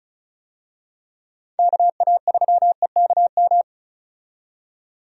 Morse code, heard as a keyed tone, about 36 wpm, 700 Hz. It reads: KA3EKM
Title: USA Call Signs - 6 Characters - 35wpm